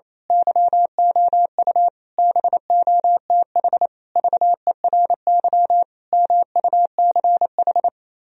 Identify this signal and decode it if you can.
Morse code, heard as a keyed tone, about 26 wpm, 700 Hz. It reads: YOU BOTH VERY MUCH
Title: Speed-Racing - Sets of 5 Words 28wpm